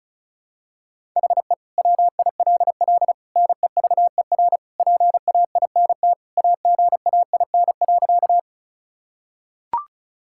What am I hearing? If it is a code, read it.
Morse code, heard as a keyed tone, about 36 wpm, 700 Hz. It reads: HE WILL NEVER PAINT AGAIN.
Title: Sentences from Top 1000 Words - Review Part 6 - 35wpm